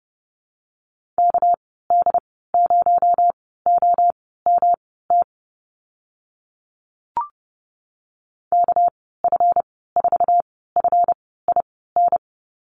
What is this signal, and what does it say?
Morse code, heard as a keyed tone, about 32 wpm, spaced as for 17 wpm, 700 Hz. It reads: KD0OMT KF4FIN
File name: Lesson 37g - Letter x - Call Signs 30wpm